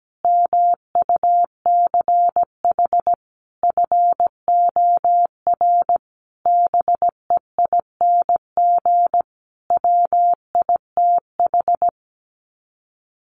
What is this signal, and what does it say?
Morse code, heard as a keyed tone, about 17 wpm, 700 Hz. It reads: MUCH FOR BEING WITH
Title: Sets of 5 Words 17wpm